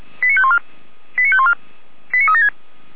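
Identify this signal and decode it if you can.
Five-tone selective call (ZVEI), heard as quick runs of five tones, tones of about 80 ms. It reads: 89514 89514 89376
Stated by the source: ZVEI five-tone selective call 89191, received on SDR.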